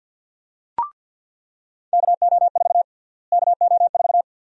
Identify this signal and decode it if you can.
Morse code, heard as a keyed tone, about 47 wpm, 700 Hz. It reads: KO4 KO4
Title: USA Call Sign Prefixes - Repeated 2x - 50wpm